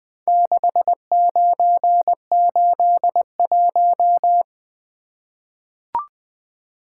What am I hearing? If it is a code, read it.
Morse code, heard as a keyed tone, about 20 wpm, 700 Hz. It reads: 6981
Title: Four Letter-Number Combinations - 20wpm